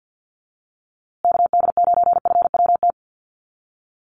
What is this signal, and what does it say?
Morse code, heard as a keyed tone, about 50 wpm, 700 Hz. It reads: KD9FWT
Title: USA Call Signs Repeated 2x 50wpm